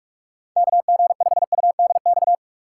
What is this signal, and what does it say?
Morse code, heard as a keyed tone, about 46 wpm, 700 Hz. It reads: KG5UDX
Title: USA Call Signs - 6 Characters - 3x Character Spacing - 45wpm